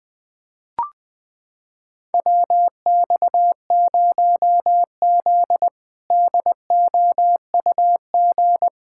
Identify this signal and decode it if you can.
Morse code, heard as a keyed tone, about 20 wpm, 700 Hz. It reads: WX0Z DOUG
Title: K1USN SST QSO Practice - 20wpm